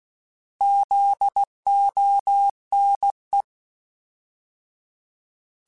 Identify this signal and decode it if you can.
Morse code, heard as a keyed tone, about 16 wpm, 800 Hz. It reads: ZONE
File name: Alphabetical / Alphabetical mp3s / Zs @15